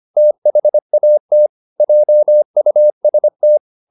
Morse code, heard as a keyed tone, about 25 wpm, 600 Hz. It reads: THAT JUST